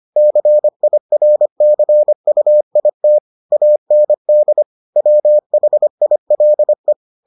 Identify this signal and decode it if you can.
Morse code, heard as a keyed tone, about 25 wpm, 600 Hz. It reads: CIRCUIT AND WHILE